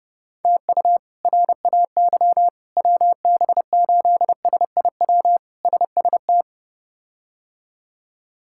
Morse code, heard as a keyed tone, about 32 wpm, 700 Hz. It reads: TU RAY WB8SIW SST
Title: K1USN SST QSO Practice - 30wpm